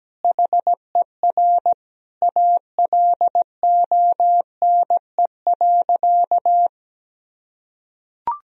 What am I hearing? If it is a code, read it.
Morse code, heard as a keyed tone, about 17 wpm, 700 Hz. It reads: HER ALONE.
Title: Sentences from Top 1000 Words - Review Part 3 - 17wpm